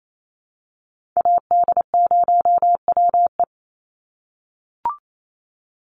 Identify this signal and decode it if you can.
Morse code, heard as a keyed tone, about 28 wpm, 700 Hz. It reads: AD0WE
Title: Call Signs - 28wpm